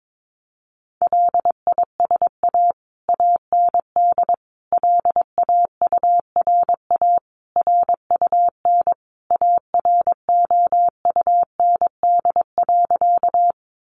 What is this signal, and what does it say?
Morse code, heard as a keyed tone, about 22 wpm, 700 Hz. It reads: LISA AND LAURA RUN AROUND.
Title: Additional Easy Phrases 22wpm